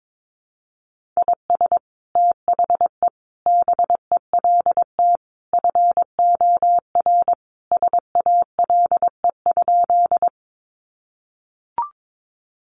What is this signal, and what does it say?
Morse code, heard as a keyed tone, about 22 wpm, 700 Hz. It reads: IS THE BELT FOR SALE?
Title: Sentences from Top 1400 Words 22wpm